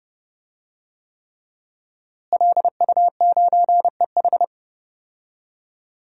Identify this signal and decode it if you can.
Morse code, heard as a keyed tone, about 30 wpm, 700 Hz. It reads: LU9EH
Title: DXCC-100 Call Signs - 5 Characters - Repeated 3x - 30wpm